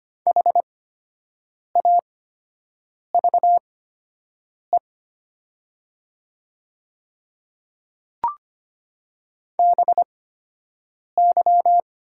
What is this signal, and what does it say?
Morse code, heard as a keyed tone, about 25 wpm, spaced as for 7 wpm, 700 Hz. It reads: HAVE BY
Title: 8x Character Spacing - Top 100 Words - 25wpm